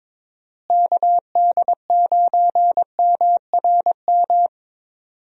Morse code, heard as a keyed tone, about 21 wpm, 700 Hz. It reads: KD9MRM
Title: USA Call Signs - 6 Characters - Repeated 2x - 22wpm